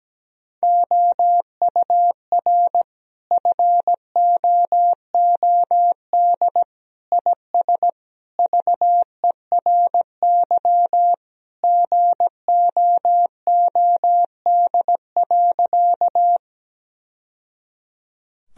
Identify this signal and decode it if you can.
Morse code, heard as a keyed tone, about 17 wpm, 700 Hz. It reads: OUR FOOD IS VERY GOOD.
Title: Sentences from Top 1000 Words - Review Part 1 - 17wpm